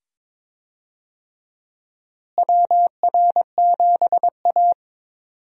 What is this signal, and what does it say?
Morse code, heard as a keyed tone, about 21 wpm, 700 Hz. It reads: WR7A
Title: USA Call Signs - 4 Characters - Repeated 2x - 22wpm